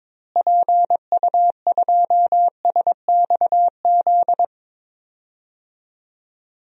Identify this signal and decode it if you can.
Morse code, heard as a keyed tone, about 22 wpm, 700 Hz. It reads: PU2SXZ
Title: DXCC-100 Call Signs - 6 Characters - Repeated 2x - 22wpm